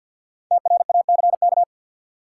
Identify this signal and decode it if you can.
Morse code, heard as a keyed tone, about 53 wpm, 700 Hz. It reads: TRACK
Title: Sentences from Top 1100 Words - New Word Drill Alt - 50wpm